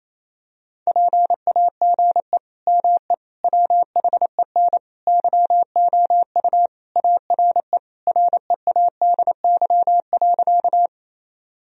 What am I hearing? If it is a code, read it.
Morse code, heard as a keyed tone, about 28 wpm, 700 Hz. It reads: PAGE ME WHEN YOU ARE READY.
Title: Sentences from Top 1000 Words - Review Part 1 - 28wpm